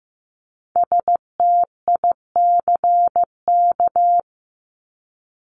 Morse code, heard as a keyed tone, about 15 wpm, 700 Hz. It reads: STICK
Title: Top 1001-1250 Words 15wpm